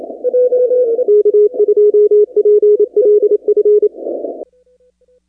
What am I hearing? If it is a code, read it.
Morse code, heard as a keyed tone, about 28 wpm, 420 Hz. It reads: K2PLF
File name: contests / 2019-cw-wpx-kt5j-k5pi- -2330 0-29